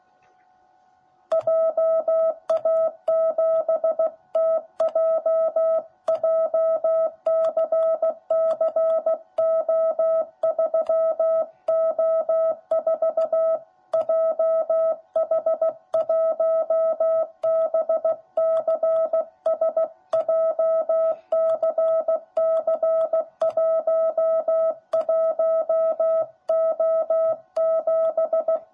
Morse code, heard as a keyed tone, about 15 wpm, 660 Hz. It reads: JA7TJ JCCO3O4 JH1BCS JCC11O7
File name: jh8cbh / 2021cw / cw10b